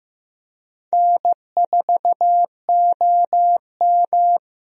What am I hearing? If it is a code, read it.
Morse code, heard as a keyed tone, about 16 wpm, 700 Hz. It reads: N4OM
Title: USA Call Signs - 4 Characters - Repeated 2x - 15wpm